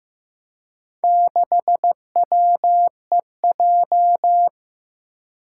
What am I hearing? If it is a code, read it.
Morse code, heard as a keyed tone, about 15 wpm, 700 Hz. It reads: 6WEJ
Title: Four Challenging Letter-Number Combinations 15wpm